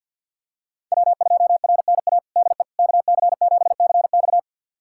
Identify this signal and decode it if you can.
Morse code, heard as a keyed tone, about 51 wpm, 690 Hz. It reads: W2RNA DE KC7CX
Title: USA Callsign DE Callsign - 5 Characters - 50wpm